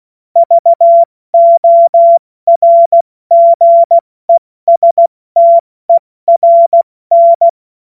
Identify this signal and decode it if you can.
Morse code, heard as a keyed tone, about 16 wpm, 680 Hz. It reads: VORGESTERN
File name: Morsekurs / Witze / Witz 13. Juni 2023 70 F80